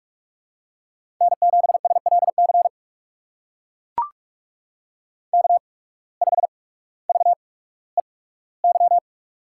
Morse code, heard as a keyed tone, about 44 wpm, spaced as for 19 wpm, 700 Hz. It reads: N7SLC K5VEY
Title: USA Call Signs - 5 Characters - 8x Character Spacing - 45wpm